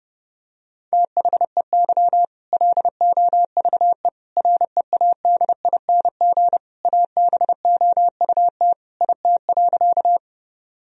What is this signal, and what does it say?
Morse code, heard as a keyed tone, about 30 wpm, 700 Hz. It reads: THEY LOVE READING ABOUT IT.
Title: Sentences from Top 1000 Words - Review Part 3 - 30wpm